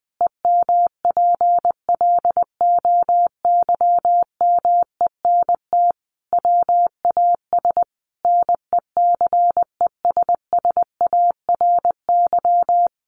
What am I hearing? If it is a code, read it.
Morse code, heard as a keyed tone, about 20 wpm, 700 Hz. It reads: EMPLOYMENT WAS NECESSARY
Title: Sentences from Top 1500 Words 20wpm